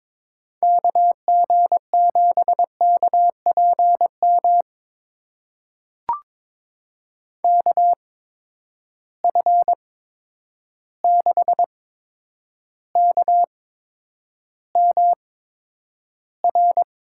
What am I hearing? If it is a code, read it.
Morse code, heard as a keyed tone, about 21 wpm, spaced as for 9 wpm, 700 Hz. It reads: KG7KPM KF6KMR
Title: USA Call Signs - 6 Characters - 8x Character Spacing - 22wpm